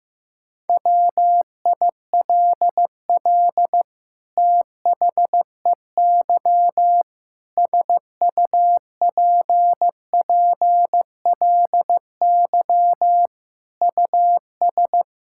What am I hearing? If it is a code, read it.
Morse code, heard as a keyed tone, about 16 wpm, 700 Hz. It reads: WILL THEY SUPPLY US
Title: Sentences from Top 1000 Words - Review Part 5 - 15wpm